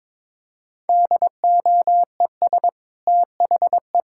Morse code, heard as a keyed tone, about 22 wpm, 700 Hz. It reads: DOES THE
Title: Sentences from Top 1600 Words 22wpm